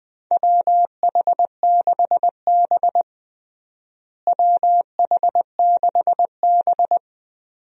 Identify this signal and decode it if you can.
Morse code, heard as a keyed tone, about 20 wpm, 700 Hz. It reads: WH6B WH6B
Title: USA Call Signs - 4 Characters - Repeated 2x - 20wpm